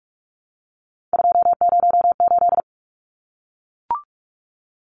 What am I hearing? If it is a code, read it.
Morse code, heard as a keyed tone, about 44 wpm, 700 Hz. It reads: 208
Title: Numbers - 3-Digit 45wpm